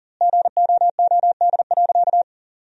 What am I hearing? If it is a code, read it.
Morse code, heard as a keyed tone, about 40 wpm, 700 Hz. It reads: GOOD.
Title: 2x Word-Spacing - Sentences from Top 500 Words - 40wpm